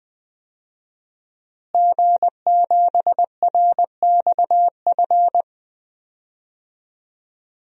Morse code, heard as a keyed tone, about 20 wpm, 700 Hz. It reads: G7RXF
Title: DXCC-100 Call Signs - 5 Characters - Repeated 2x - 20wpm